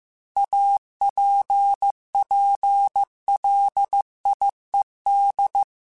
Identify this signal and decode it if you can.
Morse code, heard as a keyed tone, about 15 wpm, 800 Hz. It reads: APPLIED
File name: Technical Subjects / Technical Subjects-mp3s / XL @13